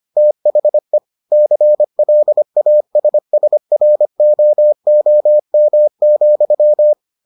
Morse code, heard as a keyed tone, about 25 wpm, 600 Hz. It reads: THE CLASSROOM,